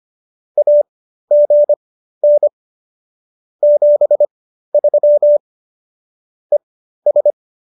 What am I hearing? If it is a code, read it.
Morse code, heard as a keyed tone, about 25 wpm, spaced as for 13 wpm, 590 Hz. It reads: AGN 73 ES